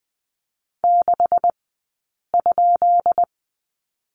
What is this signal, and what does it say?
Morse code, heard as a keyed tone, about 20 wpm, spaced as for 8 wpm, 700 Hz. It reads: 6?
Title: 197 Lesson 32d - Letter k - Two Characters 20wpm